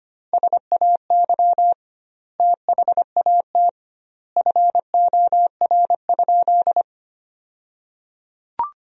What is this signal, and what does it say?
Morse code, heard as a keyed tone, about 25 wpm, 700 Hz. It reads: SAY THAT FOR?
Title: 2x Word-Spacing - Sentences from Top 200 Words - 25wpm